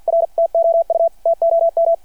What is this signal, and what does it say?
Morse code, heard as a keyed tone, about 51 wpm, 660 Hz. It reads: R TOU TOM
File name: archived / CRSnet